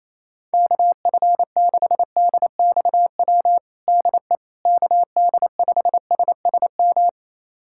Morse code, heard as a keyed tone, about 28 wpm, 700 Hz. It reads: KF6DXW DE KD5SSM
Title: USA Callsign DE Callsign - 6 Characters - 28wpm